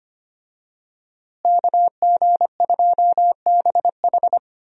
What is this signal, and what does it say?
Morse code, heard as a keyed tone, about 25 wpm, 700 Hz. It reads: KG2BH
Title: USA Call Signs - 5 Characters - Repeated 3x - 25wpm